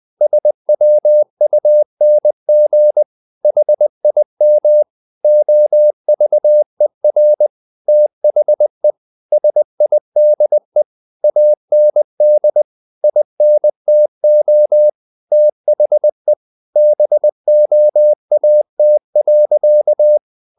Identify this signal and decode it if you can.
Morse code, heard as a keyed tone, about 20 wpm, 600 Hz. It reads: SWUNG HIM OVER THE SIDE AND INTO THE BOAT.
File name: the old man and the sea / the old man and the sea part 1 20wpm